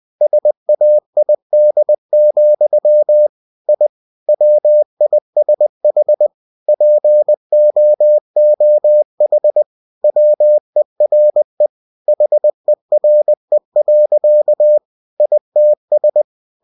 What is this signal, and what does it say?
Morse code, heard as a keyed tone, about 20 wpm, 600 Hz. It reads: SAID, I WISH POOH WERE HERE. ITS
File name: cw-books / winnie-the-pooh / winnie-20wpm-page-0093